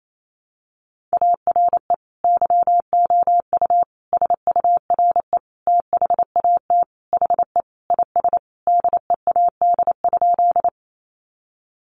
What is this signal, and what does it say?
Morse code, heard as a keyed tone, about 28 wpm, 700 Hz. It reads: ARE YOU SURE THAT HE IS DEAD?
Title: Sentences from Top 1500 Words 28wpm